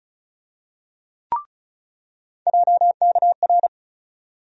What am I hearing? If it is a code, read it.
Morse code, heard as a keyed tone, about 34 wpm, 700 Hz. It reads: JKR
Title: USA Call Sign Postfixes - 3 Letters 35wpm